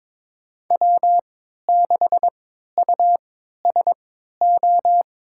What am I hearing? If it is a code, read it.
Morse code, heard as a keyed tone, about 22 wpm, spaced as for 12 wpm, 700 Hz. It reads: W6USO
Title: USA Call Signs - 5 Characters - 3x Character Spacing - 22wpm